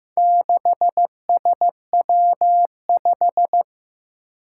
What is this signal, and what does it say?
Morse code, heard as a keyed tone, about 15 wpm, 700 Hz. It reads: 6SW5
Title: Four Challenging Letter-Number Combinations 15wpm